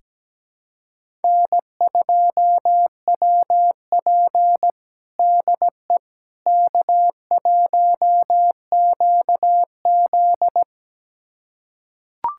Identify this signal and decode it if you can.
Morse code, heard as a keyed tone, about 17 wpm, 700 Hz. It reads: N2WP DE K1QZ
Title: USA Callsign DE Callsign - 4 Characters - 17wpm